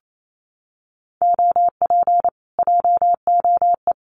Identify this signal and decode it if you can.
Morse code, heard as a keyed tone, about 28 wpm, 700 Hz. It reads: OP JOE
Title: Common QSO Phrases Encore 28wpm